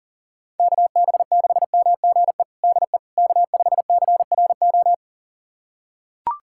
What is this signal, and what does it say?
Morse code, heard as a keyed tone, about 40 wpm, 700 Hz. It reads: KB6MGE DE K5CRO
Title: USA Callsign DE Callsign 40wpm